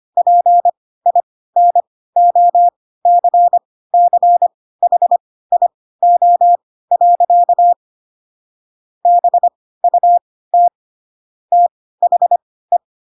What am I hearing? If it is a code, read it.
Morse code, heard as a keyed tone, about 25 wpm, spaced as for 16 wpm, 700 Hz. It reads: PINOCCHIO. BUT THE
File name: Pinocchio / Pinocchio ch4 10 25 ik1hge